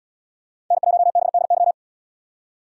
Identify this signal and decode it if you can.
Morse code, heard as a keyed tone, about 78 wpm, 700 Hz. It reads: N1BDY